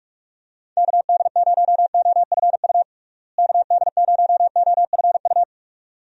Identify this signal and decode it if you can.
Morse code, heard as a keyed tone, about 41 wpm, 700 Hz. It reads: KD0OFU KD0OFU
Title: USA Call Signs - 6 Characters - Repeated 2x - 45wpm